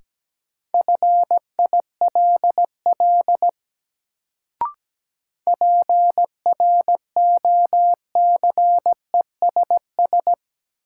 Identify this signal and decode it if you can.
Morse code, heard as a keyed tone, about 17 wpm, 700 Hz. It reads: FILL PROCESS
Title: Sentences from Top 1000 Words - Review Part 3 - New Word Drill - 17wpm